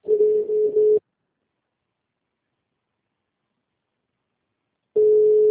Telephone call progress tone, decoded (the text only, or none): ringback tone